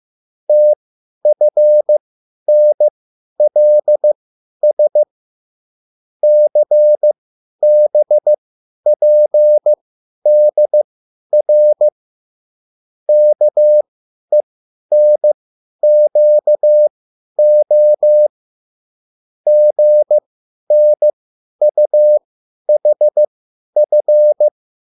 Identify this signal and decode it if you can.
Morse code, heard as a keyed tone, about 15 wpm, spaced as for 10 wpm, 600 Hz. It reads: TFNLS CBPDR KENQO GNUHF
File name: QSL / CW lesson